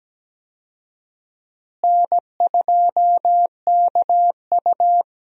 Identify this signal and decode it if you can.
Morse code, heard as a keyed tone, about 17 wpm, 700 Hz. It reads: N2KU
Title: USA Call Signs - 4 Characters - Repeated 2x - 17wpm